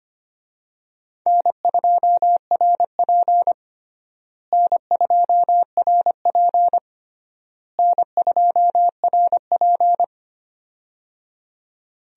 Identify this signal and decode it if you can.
Morse code, heard as a keyed tone, about 25 wpm, 700 Hz. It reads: N2RP N2RP N2RP
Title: USA Call Signs - 4 Characters - Repeated 3x - 25wpm